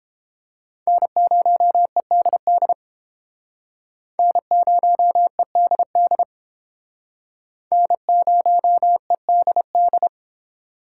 Morse code, heard as a keyed tone, about 30 wpm, 700 Hz. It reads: N0EDD N0EDD N0EDD